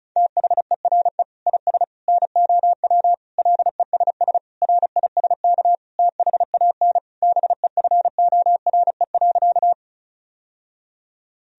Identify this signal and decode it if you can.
Morse code, heard as a keyed tone, about 34 wpm, 700 Hz. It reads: THERE IS NOW LESS RISK THAN BEFORE.
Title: Sentences from Top 1000 Words - Review Part 4 - 35wpm